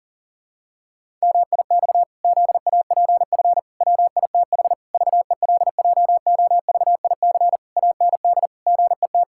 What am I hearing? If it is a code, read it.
Morse code, heard as a keyed tone, about 40 wpm, 700 Hz. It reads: MIX ZAPF WITH VELJOVIC AND GET
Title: Pangram Challenge v2 - 40wpm